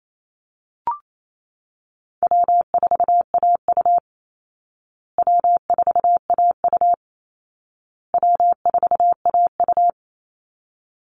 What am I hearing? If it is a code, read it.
Morse code, heard as a keyed tone, about 28 wpm, 700 Hz. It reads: W4AU W4AU W4AU
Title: Easy Call Signs Repeated 3x - 28wpm